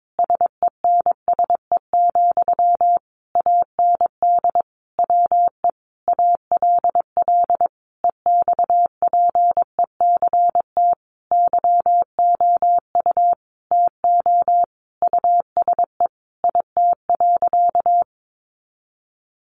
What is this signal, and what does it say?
Morse code, heard as a keyed tone, about 22 wpm, 700 Hz. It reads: SENSE, AND WE ALL EXPECT YOU TO USE IT.
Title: Sentences from Top 1100 Words 22wpm